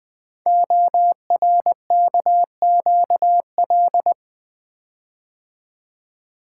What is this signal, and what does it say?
Morse code, heard as a keyed tone, about 20 wpm, 700 Hz. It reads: ORKQL
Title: Code Groups - 5 Letters - 20wpm